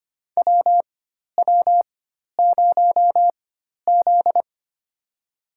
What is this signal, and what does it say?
Morse code, heard as a keyed tone, about 25 wpm, spaced as for 12 wpm, 700 Hz. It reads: WW0Z
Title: USA Call Signs - 4x Character Spacing - 25wpm